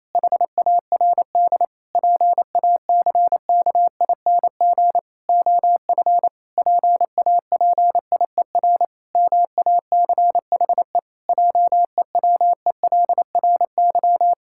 Morse code, heard as a keyed tone, about 28 wpm, 700 Hz. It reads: HARD PACKING OF PAPIER MACHE JEWELRY
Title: Pangram Challenge v2 - 28wpm